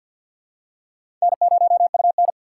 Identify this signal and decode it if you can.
Morse code, heard as a keyed tone, about 50 wpm, 700 Hz. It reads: N0UN
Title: USA Call Signs - 4 Characters - 3x Character Spacing - 50wpm